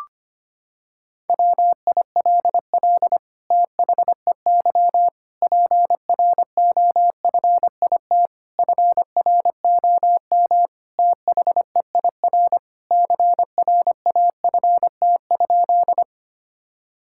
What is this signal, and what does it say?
Morse code, heard as a keyed tone, about 25 wpm, 700 Hz. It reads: WILL THEY PROFIT FROM THEIR CRAFT?
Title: Sentences from Top 1000 Words - Review Part 6 - 25wpm